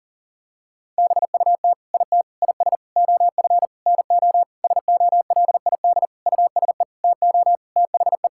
Morse code, heard as a keyed tone, about 40 wpm, 700 Hz. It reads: BUT IT IS OF NO SOLID USE TO THE
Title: Sentences from Top 1600 Words 40wpm